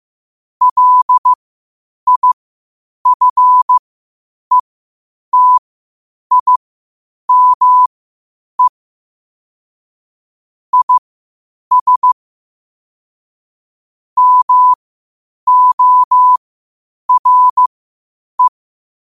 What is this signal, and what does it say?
Morse code, heard as a keyed tone, about 15 wpm, spaced as for 8 wpm, 1000 Hz. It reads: LIFETIME IS MORE
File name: qotdfiles / 8wpm